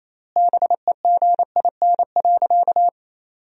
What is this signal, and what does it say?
Morse code, heard as a keyed tone, about 28 wpm, 700 Hz. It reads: BEGIN.
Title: 3x Word-Spacing - Sentences from Top 300 Words - 28wpm